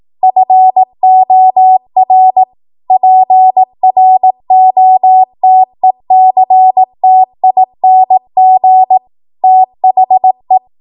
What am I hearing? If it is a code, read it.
Morse code, heard as a keyed tone, about 18 wpm, 750 Hz. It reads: FOR PROTECTING THE
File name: Morse / 18WPM